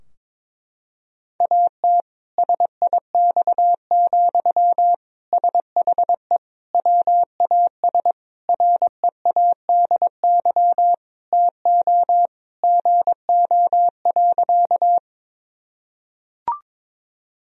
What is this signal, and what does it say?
Morse code, heard as a keyed tone, about 21 wpm, 700 Hz. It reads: AT SIX, SHE WAS READY TO GO.
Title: Sentences from Top 1000 Words - Review Part 2 - 22wpm